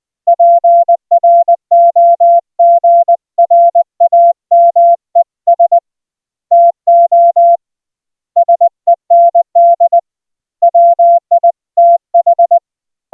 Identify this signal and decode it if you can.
Morse code, heard as a keyed tone, about 20 wpm, 690 Hz. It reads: PROGRAMES TO SEND WITH